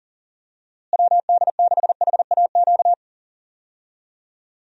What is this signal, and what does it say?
Morse code, heard as a keyed tone, about 38 wpm, 700 Hz. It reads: WD6HAQ
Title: USA Call Signs - 6 Characters - Repeated 2x - 40wpm